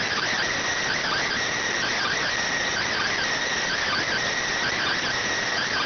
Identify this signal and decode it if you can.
Emergency Alert System (SAME) tones, heard as harsh buzzing bursts. It is heard from 0.9 to 4.9 s.